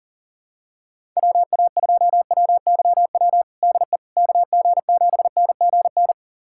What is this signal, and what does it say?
Morse code, heard as a keyed tone, about 40 wpm, 700 Hz. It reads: WA2WYW DE KG7NGN
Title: USA Callsign DE Callsign - 6 Characters - 40wpm